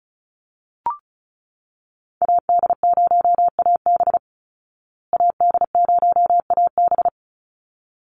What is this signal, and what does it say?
Morse code, heard as a keyed tone, about 34 wpm, 700 Hz. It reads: AD0AB AD0AB
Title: Call Signs Repeated 3x - 35wpm